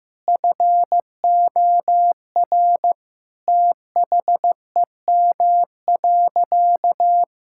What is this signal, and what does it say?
Morse code, heard as a keyed tone, about 15 wpm, 700 Hz. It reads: FOR THEM.
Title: Sentences from Top 1600 Words 15wpm